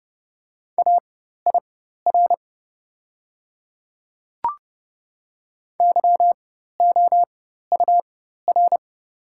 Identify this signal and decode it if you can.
Morse code, heard as a keyed tone, about 30 wpm, spaced as for 14 wpm, 690 Hz. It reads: AIR YOUR
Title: 4x Character Spacing - Top 200 Words - 30wpm